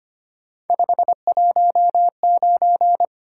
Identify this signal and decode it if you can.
Morse code, heard as a keyed tone, about 25 wpm, 700 Hz. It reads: 519
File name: Three Letter-Number Combinations - 25wpm